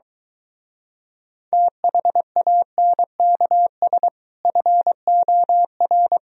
Morse code, heard as a keyed tone, about 23 wpm, 700 Hz. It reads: THANKS FOR